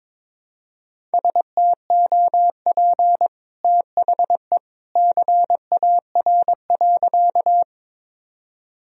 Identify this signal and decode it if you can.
Morse code, heard as a keyed tone, about 22 wpm, 700 Hz. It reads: STOP THE CAR.
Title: Sentences from Top 1000 Words - Review Part 1 - 22wpm